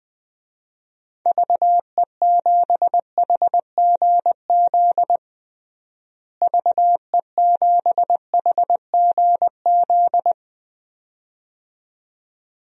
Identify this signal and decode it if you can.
Morse code, heard as a keyed tone, about 21 wpm, 700 Hz. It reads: VE7HGZ VE7HGZ
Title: DXCC-100 Call Signs - 6 Characters - Repeated 2x - 20wpm